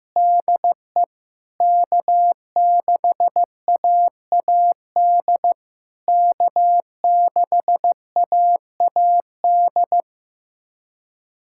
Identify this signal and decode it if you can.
Morse code, heard as a keyed tone, about 15 wpm, 700 Hz. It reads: DE K6AAD K6AAD
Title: SOTA QSO Practice 15wpm